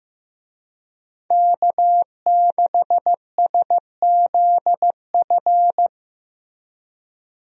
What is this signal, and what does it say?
Morse code, heard as a keyed tone, about 16 wpm, 700 Hz. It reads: K6SZF
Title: USA Call Signs - 5 Characters - Repeated 3x - 15wpm